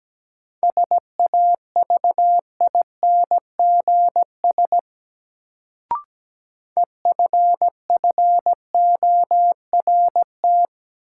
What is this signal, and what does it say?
Morse code, heard as a keyed tone, about 17 wpm, 700 Hz. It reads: SAVINGS EFFORT
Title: Sentences from Top 1200 Words - New Word Drill Alt - 17wpm